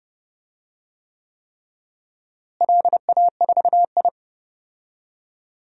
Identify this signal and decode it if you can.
Morse code, heard as a keyed tone, about 32 wpm, 700 Hz. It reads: LA4I
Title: DXCC-100 Call Signs - 4 Characters - Repeated 3x - 30wpm